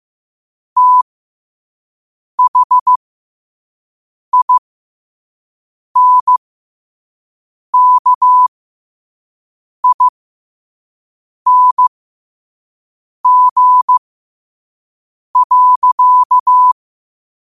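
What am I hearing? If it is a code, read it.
Morse code, heard as a keyed tone, about 15 wpm, spaced as for 5 wpm, 1000 Hz. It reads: THINKING.
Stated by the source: Morse code quote of the day sent at 5 wpm.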